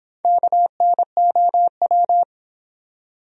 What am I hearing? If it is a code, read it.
Morse code, heard as a keyed tone, about 26 wpm, 700 Hz. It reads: KNOW